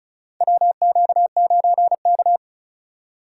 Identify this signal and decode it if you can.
Morse code, heard as a keyed tone, about 36 wpm, 700 Hz. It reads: WQ9K
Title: USA Call Signs - 4 Characters - 2x Character Spacing - 35wpm